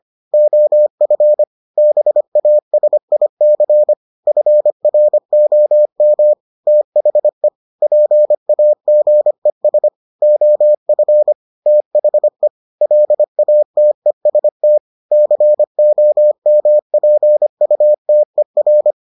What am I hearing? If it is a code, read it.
Morse code, heard as a keyed tone, about 25 wpm, 600 Hz. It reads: OF BASIC FROM THE PAGES OF THE LATEST COMPUTER